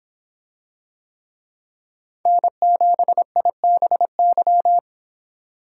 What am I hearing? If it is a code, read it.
Morse code, heard as a keyed tone, about 26 wpm, 700 Hz. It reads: N7IBY